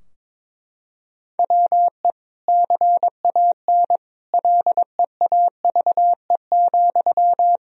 Morse code, heard as a keyed tone, about 22 wpm, 700 Hz. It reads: WE CAN LEAVE,
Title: Sentences from Top 1000 Words - Review Part 4 - 22wpm